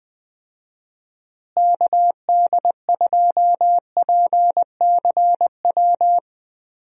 Morse code, heard as a keyed tone, about 20 wpm, 700 Hz. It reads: KD2PCW
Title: USA Call Signs - 6 Characters - Repeated 2x - 20wpm